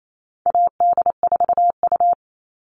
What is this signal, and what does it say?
Morse code, heard as a keyed tone, about 28 wpm, 700 Hz. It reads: AD4U
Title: All Cut Numbers - 4-Digit 28wpm